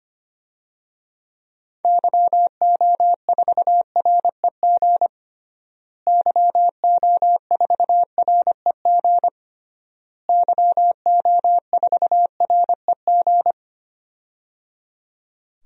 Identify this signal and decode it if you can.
Morse code, heard as a keyed tone, about 25 wpm, 700 Hz. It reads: YO4REG YO4REG YO4REG
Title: DXCC-100 Call Signs - 6 Characters - Repeated 3x - 25wpm